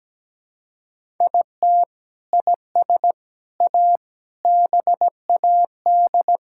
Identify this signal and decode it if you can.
Morse code, heard as a keyed tone, about 17 wpm, 700 Hz. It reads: IT IS A BAD
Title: Sentences from Top 1000 Words - Review Part 6 - 17wpm